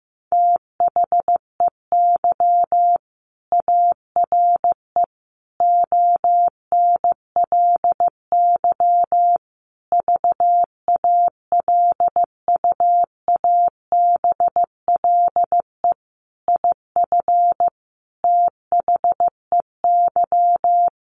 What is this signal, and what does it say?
Morse code, heard as a keyed tone, about 15 wpm, 700 Hz. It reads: THEY ARE ONLY VALUABLE IF THEY
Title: Sentences from Top 1400 Words 15wpm